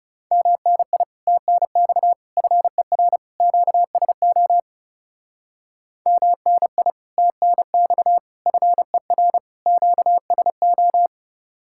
Morse code, heard as a keyed tone, about 38 wpm, 700 Hz. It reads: MNI TNX FER QSO MNI TNX FER QSO